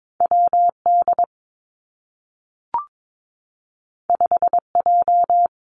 Morse code, heard as a keyed tone, about 22 wpm, 700 Hz. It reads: W D 5 J
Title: Two Challenging Letter-Number Combinations 22wpm